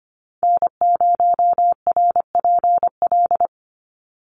Morse code, heard as a keyed tone, about 25 wpm, 700 Hz. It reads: N0RPL
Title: USA Call Signs 25wpm